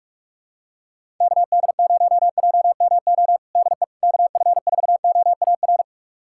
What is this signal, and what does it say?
Morse code, heard as a keyed tone, about 46 wpm, 700 Hz. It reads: KD0JMO DE KF4OAR
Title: USA Callsign DE Callsign - 6 Characters - 45wpm